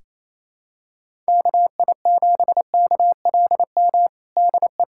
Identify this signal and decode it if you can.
Morse code, heard as a keyed tone, about 28 wpm, 700 Hz. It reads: KI7KLM DE
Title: USA Callsign DE Callsign - 6 Characters - 28wpm